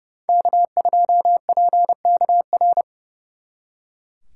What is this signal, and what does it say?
Morse code, heard as a keyed tone, about 39 wpm, 700 Hz. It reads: K2PKR